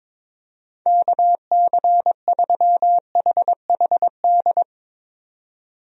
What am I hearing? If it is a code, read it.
Morse code, heard as a keyed tone, about 22 wpm, 700 Hz. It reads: KC3HHD
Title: USA Call Signs - 6 Characters - 22wpm